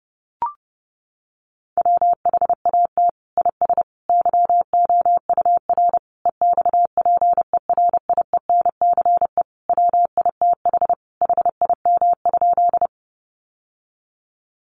Morse code, heard as a keyed tone, about 30 wpm, 700 Hz. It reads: WHAT IS YOUR EXPERIENCE WITH HIM?
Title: Sentences from Top 600 Words 30wpm